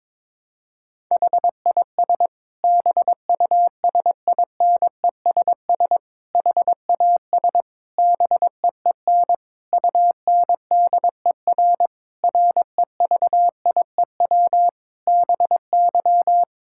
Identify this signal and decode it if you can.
Morse code, heard as a keyed tone, about 22 wpm, 700 Hz. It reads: HIS BUSINESS HAS BEEN UNDER REVIEW BY
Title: Sentences from Top 1000 Words - Review Part 6 - 22wpm